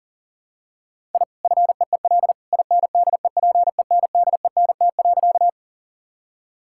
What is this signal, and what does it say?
Morse code, heard as a keyed tone, about 38 wpm, 700 Hz. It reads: I FEEL INDEPENDENT.
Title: Sentences from Top 1000 Words - Review Part 6 - 40wpm